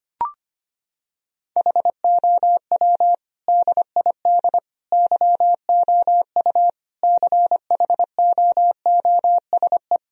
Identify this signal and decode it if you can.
Morse code, heard as a keyed tone, about 25 wpm, 700 Hz. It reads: HOW DID YOU CHOOSE
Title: Sentences from Top 1000 Words - Review Part 4 - 25wpm